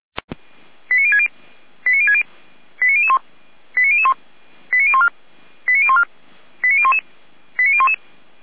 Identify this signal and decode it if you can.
Five-tone selective call (ZVEI), heard as quick runs of five tones, tones of about 70 ms. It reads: 89060 89060 89001 89001 89014 89014 89011 89011
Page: ZVEI